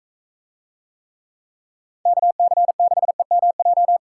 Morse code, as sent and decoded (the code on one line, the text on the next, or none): -.- -.-. -.... -- .---
KC6EMJ